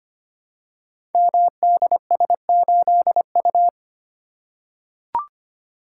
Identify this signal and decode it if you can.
Morse code, heard as a keyed tone, about 25 wpm, 700 Hz. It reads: MDS8U
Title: Code Groups - 5 Letter:Numbers - 25wpm